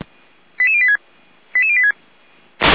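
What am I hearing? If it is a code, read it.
Five-tone selective call (ZVEI), heard as quick runs of five tones, tones of about 70 ms. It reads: 88086 88086
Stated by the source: ZVEI five-tone call received by SDR, from feuerwehr-orlamuende.org fire brigade audio server.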